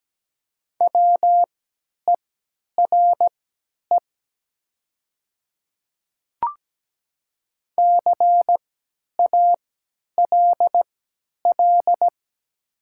Morse code, heard as a keyed tone, about 17 wpm, spaced as for 10 wpm, 700 Hz. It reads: WERE CALL
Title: 3x Character Spacing - Top 100 Words - 17wpm